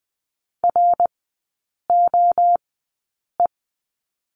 Morse code, heard as a keyed tone, about 20 wpm, spaced as for 8 wpm, 700 Hz. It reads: ROE
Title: 053 Lesson 10e - Letter r - Three Characters 20wpm